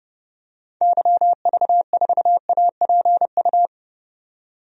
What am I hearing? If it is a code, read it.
Morse code, heard as a keyed tone, about 31 wpm, 700 Hz. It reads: YV4APU
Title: DXCC-100 Call Signs - 6 Characters - 30wpm